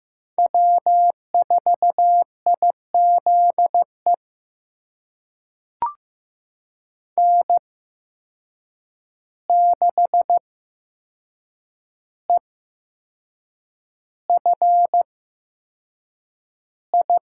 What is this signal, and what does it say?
Morse code, heard as a keyed tone, about 15 wpm, spaced as for 6 wpm, 700 Hz. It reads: W4IZE N6EFI
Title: USA Call Signs - 5 Characters - 8x Character Spacing - 15wpm